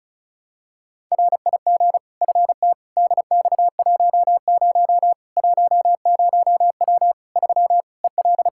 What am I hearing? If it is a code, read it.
Morse code, heard as a keyed tone, about 34 wpm, 700 Hz. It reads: RIG FT DX10 10W 3 EL
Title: K1USN SST QSO Practice 35wpm